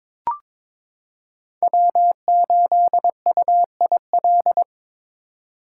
Morse code, heard as a keyed tone, about 22 wpm, 700 Hz. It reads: W8UIL
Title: K1USN SST QSO Practice - 22wpm